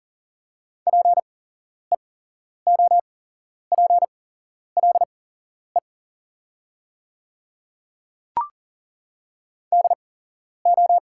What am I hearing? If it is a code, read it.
Morse code, heard as a keyed tone, about 40 wpm, spaced as for 11 wpm, 690 Hz. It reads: PEOPLE DO
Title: 8x Character Spacing - Top 200 Words - 40wpm